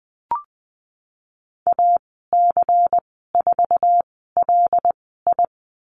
Morse code, heard as a keyed tone, about 20 wpm, spaced as for 15 wpm, 700 Hz. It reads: AC4LI
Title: 128 Lesson 21g - Number 6 - Call Signs 20wpm